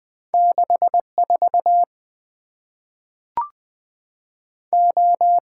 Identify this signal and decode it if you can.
Morse code, heard as a keyed tone, about 20 wpm, 700 Hz. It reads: 6 4 O
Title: Two Letter-Number Combinations - 20wpm